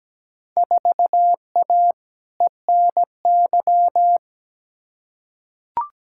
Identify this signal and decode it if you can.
Morse code, heard as a keyed tone, about 17 wpm, 700 Hz. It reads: 4A ENY
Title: Field Day Exchanges Encore - 17wpm